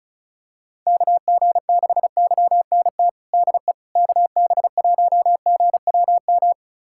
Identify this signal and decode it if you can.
Morse code, heard as a keyed tone, about 34 wpm, 700 Hz. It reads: KG6YNT DE KB1GWM
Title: USA Callsign DE Callsign - 6 Characters - 35wpm